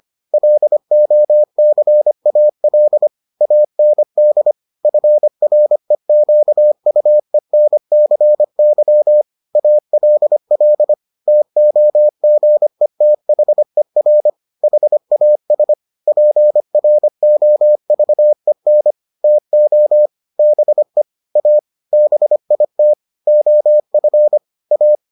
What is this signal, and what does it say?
Morse code, heard as a keyed tone, about 25 wpm, 600 Hz. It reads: LOCAL AND FREQUENCY ALL TOGETHER HAS PROVEN TO BE A BIT OF A